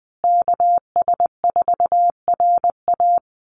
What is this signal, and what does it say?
Morse code, heard as a keyed tone, about 20 wpm, 700 Hz. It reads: KS4RA
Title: USA Call Signs Repeated 2x 20wpm